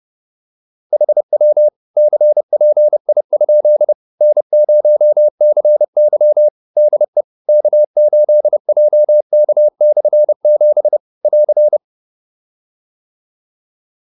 Morse code, heard as a keyed tone, about 30 wpm, 600 Hz. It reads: HW CPI? N0CY DE K8JK/7 <AR>
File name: QSO310 30